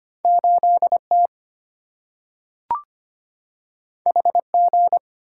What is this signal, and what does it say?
Morse code, heard as a keyed tone, about 25 wpm, 700 Hz. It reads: 8T HG
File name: Two Letter-Number Combinations - 25wpm